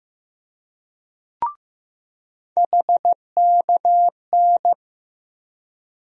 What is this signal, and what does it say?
Morse code, heard as a keyed tone, about 16 wpm, 700 Hz. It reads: HKN
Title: USA Call Sign Postfixes - 3 Letters 15wpm